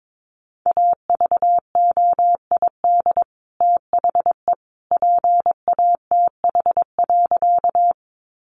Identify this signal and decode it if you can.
Morse code, heard as a keyed tone, about 22 wpm, 700 Hz. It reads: AVOID THE PATH.
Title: Sentences from Top 1400 Words 22wpm